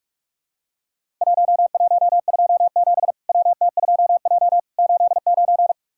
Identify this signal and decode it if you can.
Morse code, heard as a keyed tone, about 46 wpm, 700 Hz. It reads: 1127 WT2J 89
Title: Worked All Europe DX Contest - QTC Exchange - 45wpm